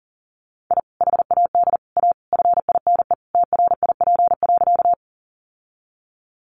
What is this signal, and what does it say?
Morse code, heard as a keyed tone, about 37 wpm, 700 Hz. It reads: I HAD A FINE TRIP.
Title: Sentences from Top 1300 Words 40wpm